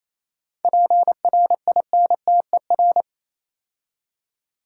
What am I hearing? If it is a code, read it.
Morse code, heard as a keyed tone, about 27 wpm, 690 Hz. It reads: PRINTER
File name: Easy Words - Speed-Racing with Hints - 22wpm